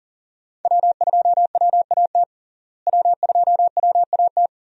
Morse code, heard as a keyed tone, about 40 wpm, 700 Hz. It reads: W2WAT W2WAT
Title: USA Call Signs - 5 Characters - Repeated 3x - 40wpm